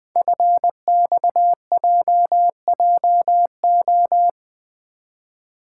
Morse code, heard as a keyed tone, about 20 wpm, 700 Hz. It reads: FXJJO
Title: Code Groups - 5 Letters - 20wpm